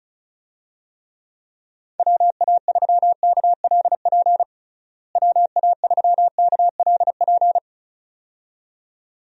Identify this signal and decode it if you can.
Morse code, heard as a keyed tone, about 34 wpm, 700 Hz. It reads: WA3KLP WA3KLP
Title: USA Call Signs - 6 Characters - Repeated 2x - 35wpm